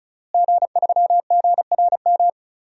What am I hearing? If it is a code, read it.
Morse code, heard as a keyed tone, about 36 wpm, 700 Hz. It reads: G3GRM
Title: DXCC-100 Call Signs - 5 Characters - Repeated 3x - 35wpm